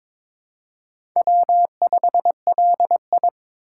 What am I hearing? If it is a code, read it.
Morse code, heard as a keyed tone, about 22 wpm, 700 Hz. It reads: W5LI
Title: USA Call Signs - 4 Characters - 4x Character Spacing - 22wpm